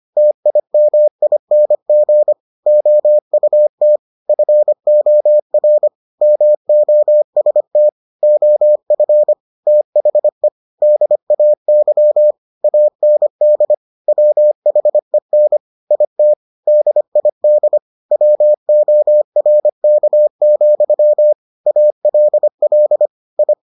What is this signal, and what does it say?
Morse code, heard as a keyed tone, about 25 wpm, 600 Hz. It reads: TIMING OUT FOR MOST OF THE DAY AND WHEN IT DID WORK, ALL I